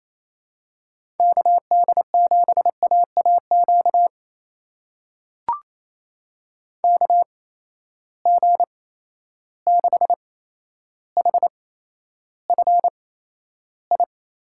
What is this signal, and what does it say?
Morse code, heard as a keyed tone, about 28 wpm, spaced as for 12 wpm, 700 Hz. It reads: KD7AAQ KG6HFI
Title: USA Call Signs - 6 Characters - 8x Character Spacing - 28wpm